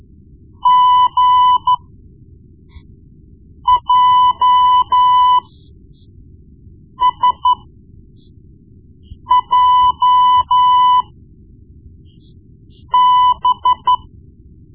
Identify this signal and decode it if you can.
Morse code, heard as a keyed tone, about 13 wpm, spaced as for 4 wpm, 1000 Hz. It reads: GJSJB
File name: Morse Lessons / Morse Lessons / Tape-1B